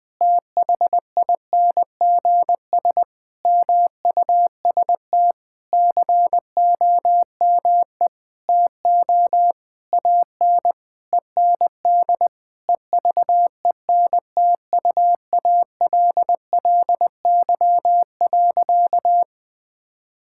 Morse code, heard as a keyed tone, about 19 wpm, 700 Hz. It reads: THINGS MUST COME TO AN END EVENTUALLY.
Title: Sentences with 8-Word Phrases - 20wpm